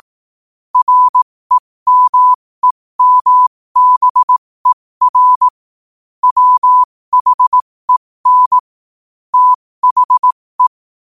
Morse code, heard as a keyed tone, about 18 wpm, spaced as for 15 wpm, 1000 Hz. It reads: REMEMBER WHEN THE
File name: qotdfiles / 15wpm